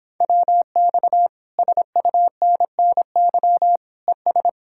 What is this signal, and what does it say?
Morse code, heard as a keyed tone, about 29 wpm, 700 Hz. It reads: WX SUNNY ES